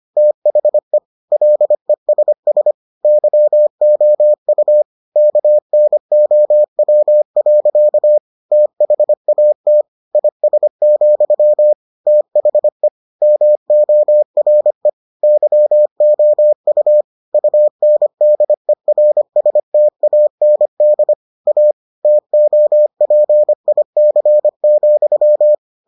Morse code, heard as a keyed tone, about 25 wpm, 600 Hz. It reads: THE LESS YOU KNOW. THAT IS, THE MORE YOU UNDERSTAND A TOPIC,